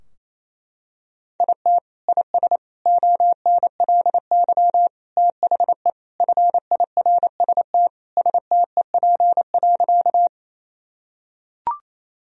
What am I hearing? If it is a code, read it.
Morse code, heard as a keyed tone, about 27 wpm, 690 Hz. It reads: IT IS ONLY THE FIRST STEP.
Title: Sentences from Top 1000 Words - Review Part 2 - 28wpm